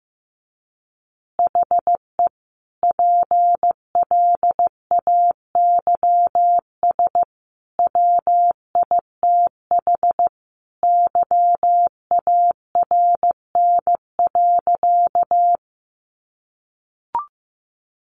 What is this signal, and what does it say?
Morse code, heard as a keyed tone, about 15 wpm, 700 Hz. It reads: HE PLAYS WITH YARN.
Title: Easy Phrases 15wpm